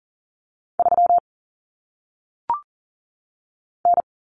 Morse code, heard as a keyed tone, about 42 wpm, 700 Hz. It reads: 3 N
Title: Common Cut Numbers - 1-Digit 40wpm